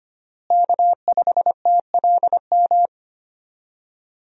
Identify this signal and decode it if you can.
Morse code, heard as a keyed tone, about 25 wpm, 700 Hz. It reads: K5TLM
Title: USA Call Signs - 5 Characters - Repeated 3x - 25wpm